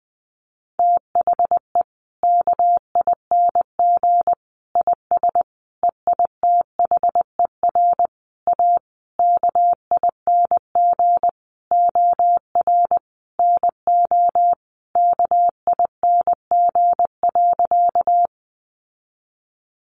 Morse code, heard as a keyed tone, about 20 wpm, 700 Hz. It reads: THE KING IS EITHER A KING OR NO KING.
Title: Sentences from Top 600 Words 20wpm